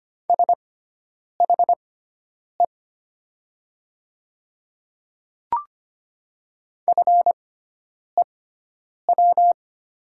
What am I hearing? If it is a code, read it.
Morse code, heard as a keyed tone, about 25 wpm, spaced as for 9 wpm, 700 Hz. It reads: SHE FEW
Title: 6x Character Spacing - Top 500 Words - 25wpm